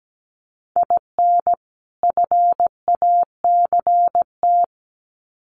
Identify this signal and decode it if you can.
Morse code, heard as a keyed tone, about 17 wpm, 700 Hz. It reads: IN FACT
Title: Sets of 2 Words 17wpm